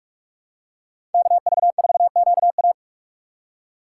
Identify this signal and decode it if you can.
Morse code, heard as a keyed tone, about 43 wpm, 700 Hz. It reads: KV4QA